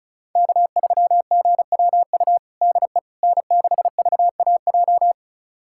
Morse code, heard as a keyed tone, about 34 wpm, 700 Hz. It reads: K3GWU DE N6VAJ
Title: USA Callsign DE Callsign - 5 Characters - 35wpm